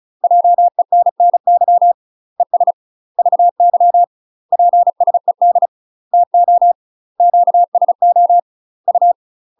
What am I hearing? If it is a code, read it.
Morse code, heard as a keyed tone, about 36 wpm, 700 Hz. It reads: JENNY ES VY PSED TO QSO U
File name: UK MASTER QSOs TRAINING / 35WPM / UK MASTER QSOs 0029